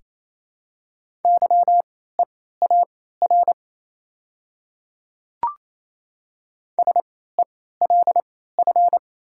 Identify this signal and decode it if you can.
Morse code, heard as a keyed tone, about 27 wpm, spaced as for 16 wpm, 700 Hz. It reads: YEAR SELF
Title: 3x Character Spacing - Top 300 Words - 28wpm